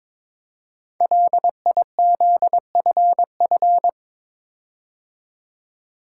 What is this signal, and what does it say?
Morse code, heard as a keyed tone, about 23 wpm, 690 Hz. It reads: LIZFF
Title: Code Groups - 5 Letters - 22wpm